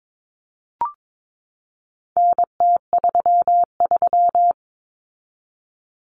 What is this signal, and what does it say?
Morse code, heard as a keyed tone, about 22 wpm, 700 Hz. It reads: NT33
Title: Common Cut Numbers - 4-Digit 22wpm